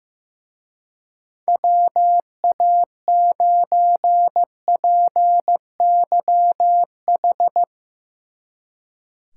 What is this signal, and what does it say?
Morse code, heard as a keyed tone, about 16 wpm, 700 Hz. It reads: WA9PYH
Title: Speed Racing - USA 2x3 Call Signs 15wpm